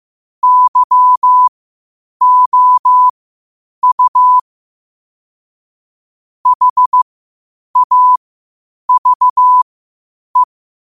Morse code, heard as a keyed tone, about 15 wpm, spaced as for 8 wpm, 1000 Hz. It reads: YOU HAVE
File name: qotdfiles / 8wpm